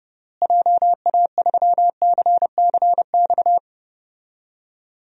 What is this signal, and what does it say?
Morse code, heard as a keyed tone, about 30 wpm, 700 Hz. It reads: JA3CCX
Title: DXCC-100 Call Signs - 6 Characters - Repeated 3x - 30wpm